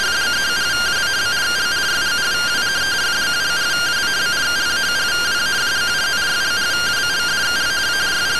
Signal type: two-tone teleprinter signal

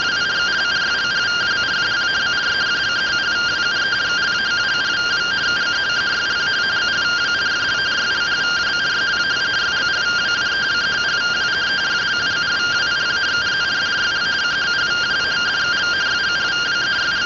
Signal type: two-tone teleprinter signal

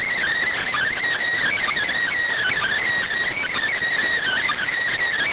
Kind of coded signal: Emergency Alert System (SAME) tones

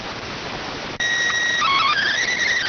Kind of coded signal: SSTV (slow-scan television)

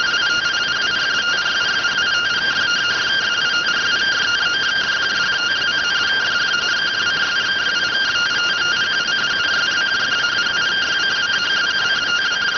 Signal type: two-tone teleprinter signal

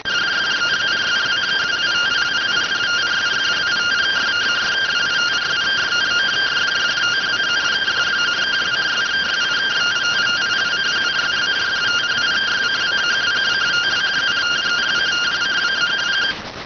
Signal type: two-tone teleprinter signal

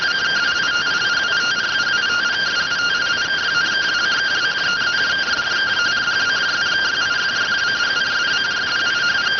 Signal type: two-tone teleprinter signal

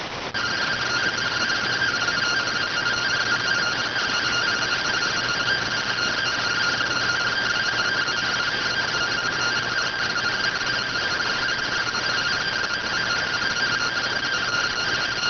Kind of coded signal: two-tone teleprinter signal

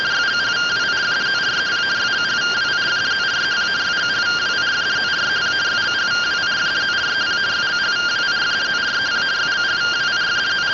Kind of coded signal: two-tone teleprinter signal

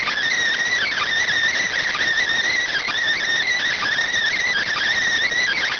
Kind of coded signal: Emergency Alert System (SAME) tones

